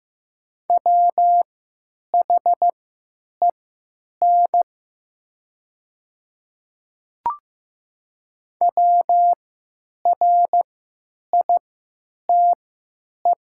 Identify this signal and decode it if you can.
Morse code, heard as a keyed tone, about 15 wpm, spaced as for 9 wpm, 700 Hz. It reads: WHEN WRITE
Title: 3x Character Spacing - Top 100 Words - 15wpm